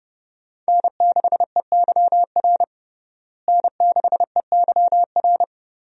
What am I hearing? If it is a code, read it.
Morse code, heard as a keyed tone, about 30 wpm, 700 Hz. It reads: N6EYR N6EYR
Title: USA Call Signs - 5 Characters - Repeated 3x - 30wpm